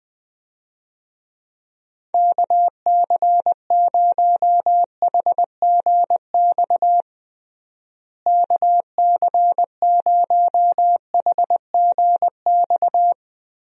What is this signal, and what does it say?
Morse code, heard as a keyed tone, about 20 wpm, 700 Hz. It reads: KC0HGX KC0HGX
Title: USA Call Signs - 6 Characters - Repeated 3x - 20wpm